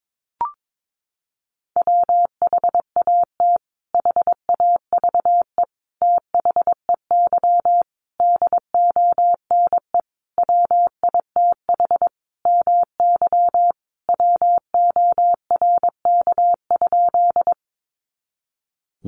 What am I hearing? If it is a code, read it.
Morse code, heard as a keyed tone, about 22 wpm, 700 Hz. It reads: WHAT HAVE THEY DONE WITH MY WORK?
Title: Sentences from Top 500 Words 22wpm